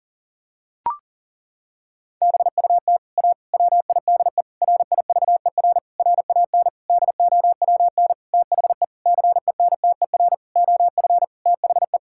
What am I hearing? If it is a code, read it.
Morse code, heard as a keyed tone, about 40 wpm, 700 Hz. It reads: BUT A WIDE RIVER RAN DOWN THE CENTER OF THE
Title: Sentences from Top 1000 Words - Review Part 6 - 40wpm